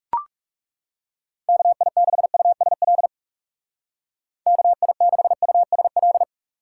Morse code, heard as a keyed tone, about 40 wpm, 700 Hz. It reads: KI6USL KI6USL